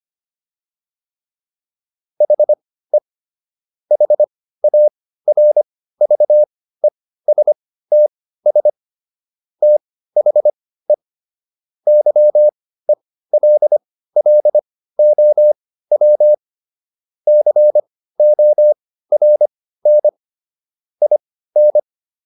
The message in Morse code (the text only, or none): HE HARVESTS THE YELLOW CORN IN